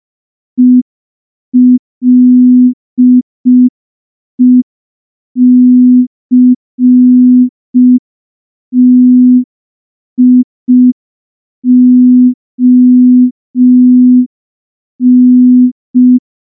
Morse code, heard as a keyed tone, about 5 wpm, 250 Hz. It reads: ELECTION
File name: morse / morse-training-05WPM